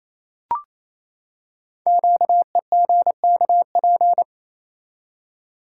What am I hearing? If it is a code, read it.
Morse code, heard as a keyed tone, about 29 wpm, 700 Hz. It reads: QEGKP
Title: Code Groups - 5 Letters - 28wpm